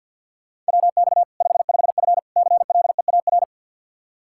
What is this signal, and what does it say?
Morse code, heard as a keyed tone, about 49 wpm, 700 Hz. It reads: WX 55F CLEAR
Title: Practice QSO - Exchange 2 - Info - 50wpm